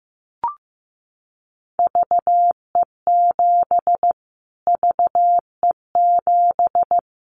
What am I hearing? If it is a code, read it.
Morse code, heard as a keyed tone, about 15 wpm, 700 Hz. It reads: VE7 VE7
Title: Callsign Prefixes for Top 25 DXCC - Repeated 3x - 15wpm